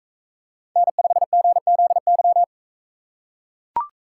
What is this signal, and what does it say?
Morse code, heard as a keyed tone, about 44 wpm, 700 Hz. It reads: N5GZY